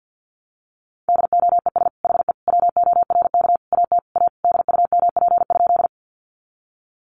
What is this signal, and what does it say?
Morse code, heard as a keyed tone, about 50 wpm, 700 Hz. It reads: DOES HE WORK AT A DUMP?
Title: Sentences from Top 1300 Words 50wpm